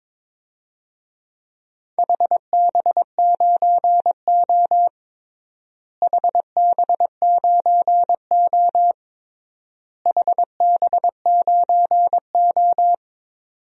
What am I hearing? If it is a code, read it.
Morse code, heard as a keyed tone, about 21 wpm, 700 Hz. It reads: HB9O HB9O HB9O
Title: DXCC-100 Call Signs - 4 Characters - Repeated 3x - 22wpm